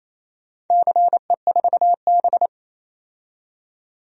Morse code, heard as a keyed tone, about 28 wpm, 700 Hz. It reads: CE4B
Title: DXCC-100 Call Signs - 4 Characters - 28wpm